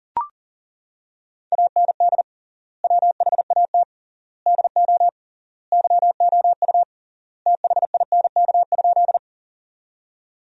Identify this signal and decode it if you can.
Morse code, heard as a keyed tone, about 40 wpm, 700 Hz. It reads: AND WHAT DO YOU THINK?
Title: 3x Word-Spacing - Sentences from Top 200 Words - 40wpm